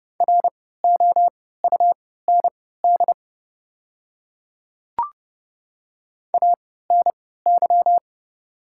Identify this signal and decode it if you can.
Morse code, heard as a keyed tone, about 30 wpm, spaced as for 17 wpm, 700 Hz. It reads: ROUND ANY
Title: 3x Character Spacing - Top 300 Words - 30wpm